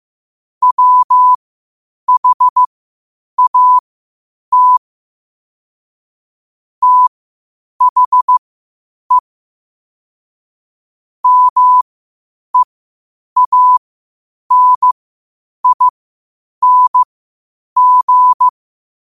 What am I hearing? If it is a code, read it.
Morse code, heard as a keyed tone, about 15 wpm, spaced as for 8 wpm, 1000 Hz. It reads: WHAT THE MEANING